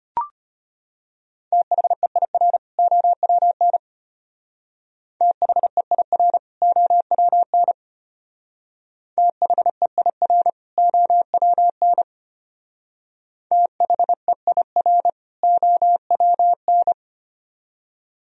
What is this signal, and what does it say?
Morse code, heard as a keyed tone, about 30 wpm, 700 Hz. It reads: THEIR OWN THEIR OWN THEIR OWN THEIR OWN
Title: Speed-Racing - Sets of 2 Words 25wpm